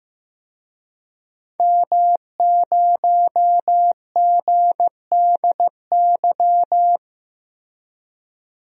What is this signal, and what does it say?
Morse code, heard as a keyed tone, about 15 wpm, 700 Hz. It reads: M0GDY
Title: DXCC-100 Call Signs - 5 Characters - Repeated 2x - 15wpm